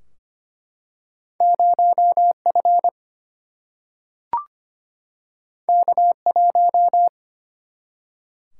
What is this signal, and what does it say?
Morse code, heard as a keyed tone, about 25 wpm, 700 Hz. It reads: 0 F K 1
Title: Single-Letter Single-Number 25wpm